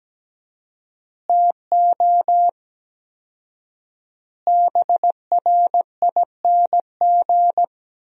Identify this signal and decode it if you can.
Morse code, heard as a keyed tone, about 17 wpm, 700 Hz. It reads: TO BRING
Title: 4x Word-Spacing - Sentences from Top 500 Words - 17wpm